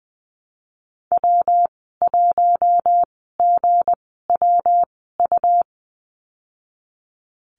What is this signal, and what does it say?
Morse code, heard as a keyed tone, about 20 wpm, spaced as for 15 wpm, 700 Hz. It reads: W1GWU
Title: 173 Lesson 28g - Number 9 - Call Signs 20wpm